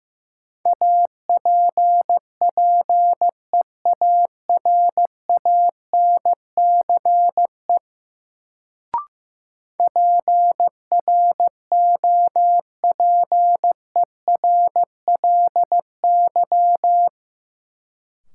Morse code, heard as a keyed tone, about 15 wpm, 700 Hz. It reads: APPEARANCE PROPERLY
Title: Sentences from Top 1100 Words - New Word Drill - 15wpm